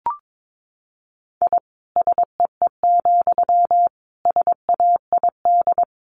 Morse code, heard as a keyed tone, about 22 wpm, 700 Hz. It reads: I SEE, SAID
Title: A Sound Of Thunder 22wpm